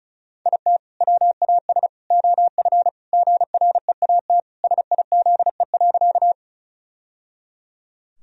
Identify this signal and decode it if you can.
Morse code, heard as a keyed tone, about 34 wpm, 700 Hz. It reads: IT WAS OF GREAT SIZE.
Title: Sentences from Top 1000 Words - Review Part 2 - 35wpm